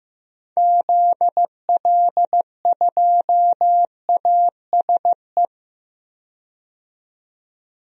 Morse code, heard as a keyed tone, about 15 wpm, 700 Hz. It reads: ZL2ASE
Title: DXCC-100 Call Signs - 6 Characters - 15wpm